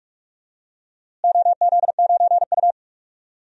Morse code, heard as a keyed tone, about 41 wpm, 700 Hz. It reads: OZ9U
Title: DXCC-100 Call Signs - 4 Characters - Repeated 3x - 45wpm